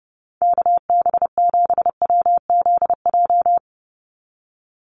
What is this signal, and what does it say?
Morse code, heard as a keyed tone, about 28 wpm, 700 Hz. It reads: KB7WZJ
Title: USA Call Signs 30wpm